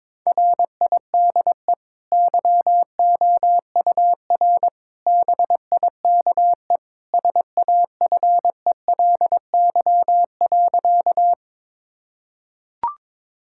Morse code, heard as a keyed tone, about 22 wpm, 700 Hz. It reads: RIDE YOUR BIKE SAFELY.
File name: practice / 22wpm / Easy Sentences with Hints - 22wpm